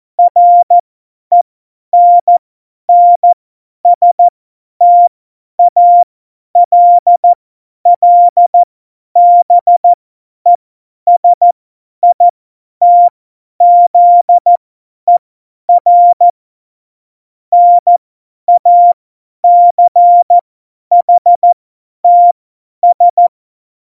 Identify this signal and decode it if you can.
Morse code, heard as a keyed tone, about 14 wpm, spaced as for 10 wpm, 700 Hz. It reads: RENNSTALLBESITZER NACHTS
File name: Morsekurs / Witze / Witz 18 August 2020, 50 F70